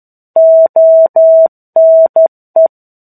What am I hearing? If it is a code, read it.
Morse code, heard as a keyed tone, about 12 wpm, 650 Hz. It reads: ONE
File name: 01-Chapter-01 12 WPM